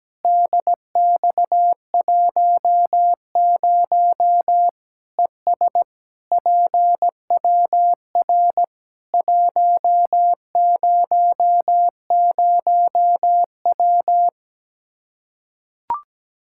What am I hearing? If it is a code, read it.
Morse code, heard as a keyed tone, about 17 wpm, 700 Hz. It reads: DX10 ES PWR 100W
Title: Practice QSO - Exchange 2 - Info - 17wpm